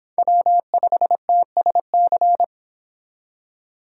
Morse code, heard as a keyed tone, about 26 wpm, 700 Hz. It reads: W5TSC